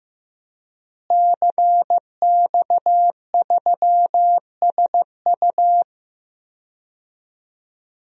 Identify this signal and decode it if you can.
Morse code, heard as a keyed tone, about 15 wpm, 700 Hz. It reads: CX3SU
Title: DXCC-100 Call Signs - 5 Characters - Repeated 3x - 15wpm